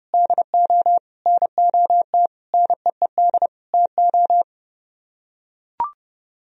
Morse code, heard as a keyed tone, about 31 wpm, 700 Hz. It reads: DO NOT NEED TO